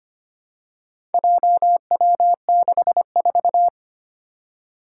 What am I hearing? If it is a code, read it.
Morse code, heard as a keyed tone, about 25 wpm, 700 Hz. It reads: JW64
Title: Four Challenging Letter-Number Combinations 25wpm